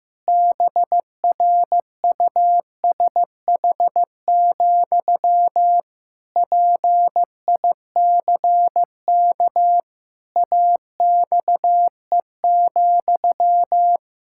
Morse code, heard as a keyed tone, about 15 wpm, 700 Hz. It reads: BRUSH, PICK AXE,
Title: Pangram Challenge v2 - 15wpm